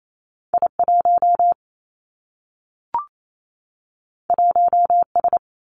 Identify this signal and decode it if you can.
Morse code, heard as a keyed tone, about 28 wpm, 700 Hz. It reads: I 1 1 S
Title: Two Challenging Letter-Number Combinations 28wpm